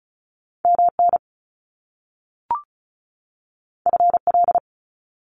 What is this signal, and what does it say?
Morse code, heard as a keyed tone, about 35 wpm, 700 Hz. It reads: MN FL
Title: US States 35wpm